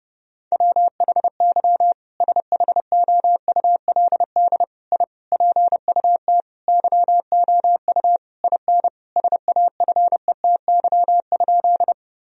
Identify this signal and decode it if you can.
Morse code, heard as a keyed tone, about 30 wpm, 700 Hz. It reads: WHY SHOULD I PUT YOU IN SAFETY?
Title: Sentences from Top 1000 Words - Review Part 4 - 30wpm